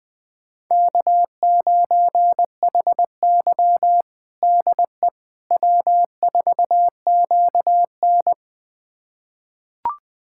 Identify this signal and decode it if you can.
Morse code, heard as a keyed tone, about 20 wpm, 700 Hz. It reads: K9HY DE W4QN
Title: USA Callsign DE Callsign - 4 Characters - 20wpm